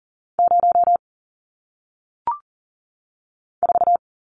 Morse code, heard as a keyed tone, about 39 wpm, 700 Hz. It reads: 0 4
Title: Numbers - 1-Digit 40wpm